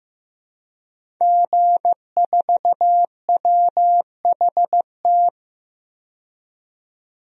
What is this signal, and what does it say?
Morse code, heard as a keyed tone, about 15 wpm, 700 Hz. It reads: G4WHT
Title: DXCC-100 Call Signs - 5 Characters - 15wpm